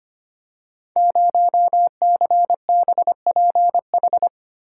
Morse code, heard as a keyed tone, about 25 wpm, 700 Hz. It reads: 0CBPH
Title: Code Groups - 5 Letter:Numbers - 25wpm